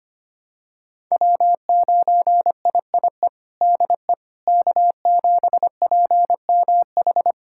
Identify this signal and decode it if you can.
Morse code, heard as a keyed tone, about 25 wpm, 700 Hz. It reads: W9IIE DE K7PMH
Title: USA Callsign DE Callsign - 5 Characters - 25wpm